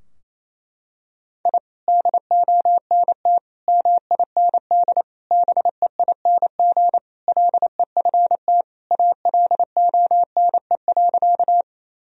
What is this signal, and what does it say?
Morse code, heard as a keyed tone, about 28 wpm, 700 Hz. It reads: I DONT MIND BEING LEFT ALONE.
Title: Sentences from Top 1000 Words - Review Part 3 - 28wpm